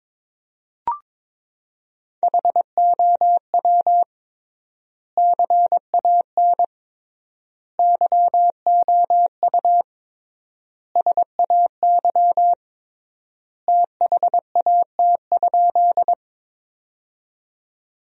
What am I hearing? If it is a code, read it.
Morse code, heard as a keyed tone, about 22 wpm, 700 Hz. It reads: HOW CAN YOU SAY THAT?
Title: 3x Word-Spacing - Sentences from Top 200 Words - 22wpm